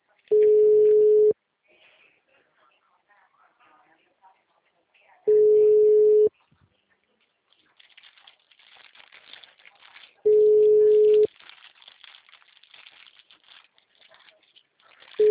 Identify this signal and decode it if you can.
Ringback tone (Europe, 425 Hz), with speech.
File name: out- -2005- - - .33069